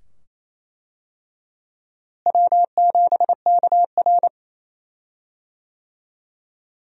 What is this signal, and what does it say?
Morse code, heard as a keyed tone, about 28 wpm, 700 Hz. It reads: W7KR
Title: USA Call Signs - 4 Characters - Repeated 2x - 28wpm